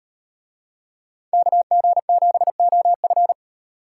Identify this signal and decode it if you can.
Morse code, heard as a keyed tone, about 38 wpm, 700 Hz. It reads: KG7OF